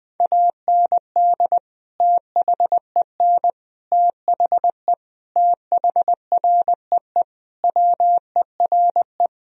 Morse code, heard as a keyed tone, about 21 wpm, 700 Hz. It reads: AND THEN THE THREE WERE
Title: Sentences from Top 1000 Words - Review Part 1 - 20wpm